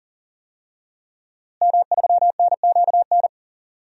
Morse code, heard as a keyed tone, about 39 wpm, 700 Hz. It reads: M3NQN
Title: DXCC-100 Call Signs - 5 Characters - Repeated 2x - 40wpm